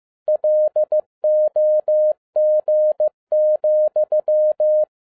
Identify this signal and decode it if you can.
Morse code, heard as a keyed tone, about 15 wpm, 610 Hz. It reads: LOG,
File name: 2011-10-mp3 / n12-JL3YMV 15wpm